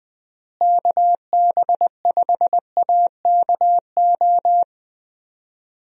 Morse code, heard as a keyed tone, about 21 wpm, 700 Hz. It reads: KB5AKO
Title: USA Call Signs - 6 Characters - Repeated 2x - 20wpm